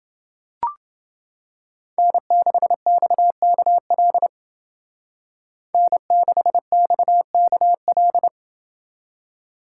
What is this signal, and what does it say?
Morse code, heard as a keyed tone, about 28 wpm, 700 Hz. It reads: N6XKL N6XKL